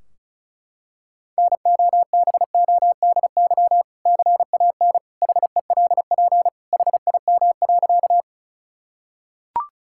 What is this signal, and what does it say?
Morse code, heard as a keyed tone, about 36 wpm, 700 Hz. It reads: NOBODY CAN HELP HIM.
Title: Sentences from Top 1600 Words 35wpm